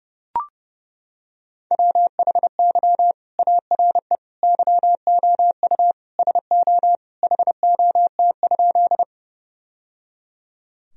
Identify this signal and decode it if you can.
Morse code, heard as a keyed tone, about 30 wpm, 700 Hz. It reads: WHY ARE YOU SO HOT?
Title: Sentences from Top 1000 Words - Review Part 3 - 30wpm